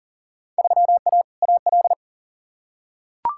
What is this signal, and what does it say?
Morse code, heard as a keyed tone, about 40 wpm, 700 Hz. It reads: 3A AL
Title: Field Day Exchanges - 40wpm